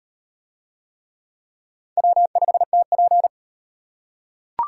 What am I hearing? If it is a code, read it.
Morse code, heard as a keyed tone, about 34 wpm, 700 Hz. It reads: W5TP